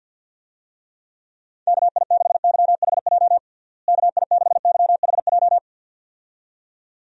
Morse code, heard as a keyed tone, about 46 wpm, 700 Hz. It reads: KI6YHJ KI6YHJ
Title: USA Call Signs - 6 Characters - Repeated 2x - 50wpm